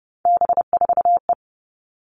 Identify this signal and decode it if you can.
Morse code, heard as a keyed tone, about 30 wpm, 700 Hz. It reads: B4E
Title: All Cut Numbers - 3-Digit 30wpm